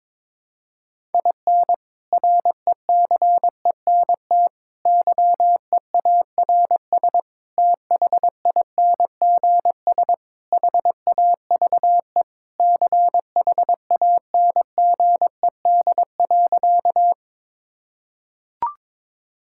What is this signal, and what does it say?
Morse code, heard as a keyed tone, about 22 wpm, 700 Hz. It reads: IN RECENT YEARS THINGS HAVE CHANGED.
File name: practice / 22wpm / Sentences with 3-Word Phrases - 22wpm